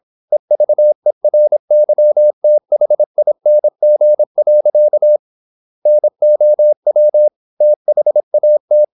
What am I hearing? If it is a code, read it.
Morse code, heard as a keyed tone, about 26 wpm, 600 Hz. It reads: EVERYTHING. NOW THAT
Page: Never 26wpm | CW med Gnister